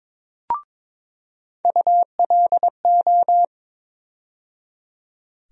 Morse code, heard as a keyed tone, about 22 wpm, 700 Hz. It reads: ULO
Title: USA Call Sign Postfixes - 3 Letters 22wpm